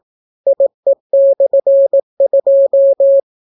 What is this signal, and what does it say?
Morse code, heard as a keyed tone, about 18 wpm, 550 Hz. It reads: IE/2